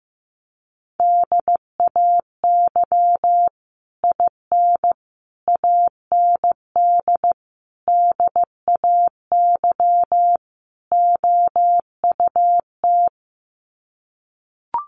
Morse code, heard as a keyed tone, about 15 wpm, 700 Hz. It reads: DAY IN AND DAY OUT
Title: Sets of 5 Words Encore2 15wpm